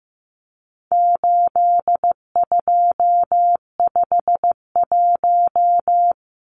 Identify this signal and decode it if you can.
Morse code, heard as a keyed tone, about 15 wpm, 700 Hz. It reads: 8251
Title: Numbers - 4-Digit 15wpm